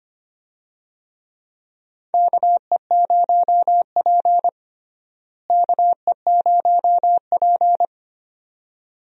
Morse code, heard as a keyed tone, about 25 wpm, 700 Hz. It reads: KE0P KE0P
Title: USA Call Signs - 4 Characters - Repeated 2x - 25wpm